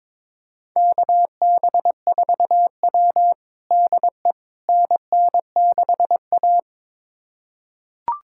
Morse code, heard as a keyed tone, about 22 wpm, 700 Hz. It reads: KB4W DE NN6A
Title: USA Callsign DE Callsign - 4 Characters - 22wpm